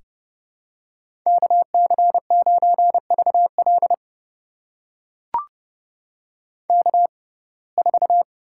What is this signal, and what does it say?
Morse code, heard as a keyed tone, about 30 wpm, spaced as for 22 wpm, 700 Hz. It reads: KC9VL K4
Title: USA Call Signs - 5 Characters - 6x Character Spacing - 30wpm